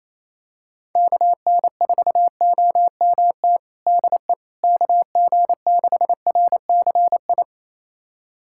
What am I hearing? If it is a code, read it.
Morse code, heard as a keyed tone, about 28 wpm, 700 Hz. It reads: KN4OMT DE KG6RCI
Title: USA Callsign DE Callsign - 6 Characters - 28wpm